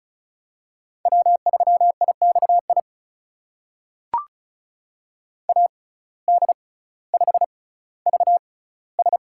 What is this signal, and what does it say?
Morse code, heard as a keyed tone, about 36 wpm, spaced as for 18 wpm, 700 Hz. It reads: W3IXI AD5VS
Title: USA Call Signs - 5 Characters - 6x Character Spacing - 35wpm